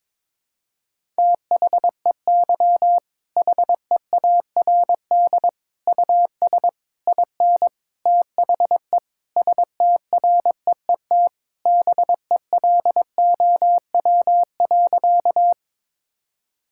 Morse code, heard as a keyed tone, about 22 wpm, 700 Hz. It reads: THEY HEARD US IN THE STREET BELOW.
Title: Sentences from Top 1000 Words - Review Part 5 - 22wpm